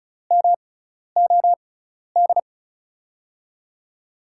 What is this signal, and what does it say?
Morse code, heard as a keyed tone, about 36 wpm, spaced as for 12 wpm, 700 Hz. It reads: MOD
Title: USA Call Signs - 6x Character Spacing - 35wpm